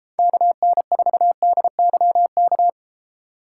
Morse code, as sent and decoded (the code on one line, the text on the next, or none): -.- -. ....- -.. -.-- -.-
KN4DYK